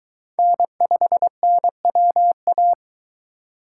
Morse code, heard as a keyed tone, about 23 wpm, 700 Hz. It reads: N5NWA